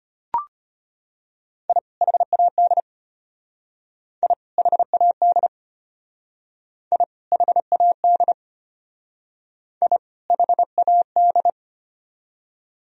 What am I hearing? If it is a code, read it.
Morse code, heard as a keyed tone, about 32 wpm, 700 Hz. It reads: I HAD I HAD I HAD I HAD